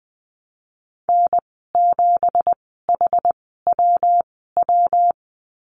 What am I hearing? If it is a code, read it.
Morse code, heard as a keyed tone, about 20 wpm, spaced as for 14 wpm, 700 Hz. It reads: N7HWW
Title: 179 Lesson 29g - Slash - Call Signs 20wpm